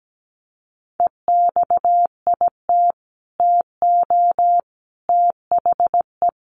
Morse code, heard as a keyed tone, about 17 wpm, 700 Hz. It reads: EXIT TO THE
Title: Sentences from Top 1100 Words 17wpm